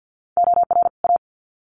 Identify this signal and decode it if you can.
Morse code, heard as a keyed tone, about 50 wpm, 700 Hz. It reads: OR A
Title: Sets of 2 Words 50wpm